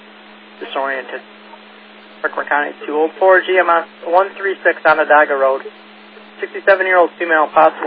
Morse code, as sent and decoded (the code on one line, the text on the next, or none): none